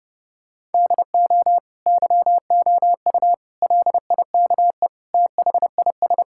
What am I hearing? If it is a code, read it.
Morse code, heard as a keyed tone, about 32 wpm, 700 Hz. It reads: DO YOU LIKE THIS
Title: Sentences from Top 1000 Words - Review Part 5 - 30wpm